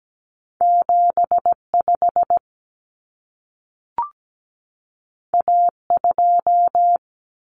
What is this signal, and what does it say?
Morse code, heard as a keyed tone, about 17 wpm, 700 Hz. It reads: 75 A2
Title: Common Cut Numbers - 2-Digit 17wpm